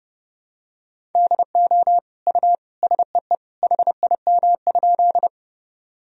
Morse code, heard as a keyed tone, about 30 wpm, 700 Hz. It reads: DO U SEE HIM?
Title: Sentences from Top 100 Words and Abbreviations 30wpm